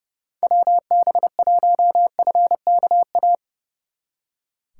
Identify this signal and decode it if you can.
Morse code, heard as a keyed tone, about 37 wpm, 700 Hz. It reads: WB1FKA
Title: Speed Racing - USA Call Signs 30wpm